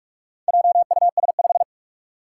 Morse code, heard as a keyed tone, about 43 wpm, 700 Hz. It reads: JUS5
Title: Four Challenging Letter-Number Combinations 45wpm